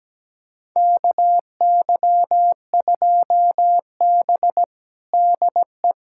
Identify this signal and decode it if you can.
Morse code, heard as a keyed tone, about 17 wpm, 700 Hz. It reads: KY2B DE
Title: USA Callsign DE Callsign - 4 Characters - 17wpm